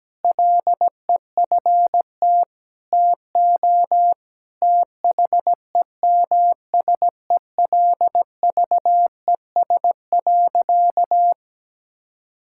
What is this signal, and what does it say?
Morse code, heard as a keyed tone, about 17 wpm, 700 Hz. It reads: LEFT TO THEMSELVES.
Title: Sentences from Top 1000 Words - Review Part 5 - 17wpm